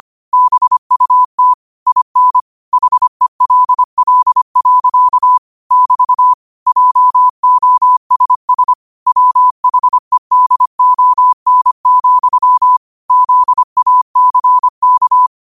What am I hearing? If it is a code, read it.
Morse code, heard as a keyed tone, about 25 wpm, 1000 Hz. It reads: BUT IN HELL. <BT> JOSS WHEDON, ZACK
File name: 25wpm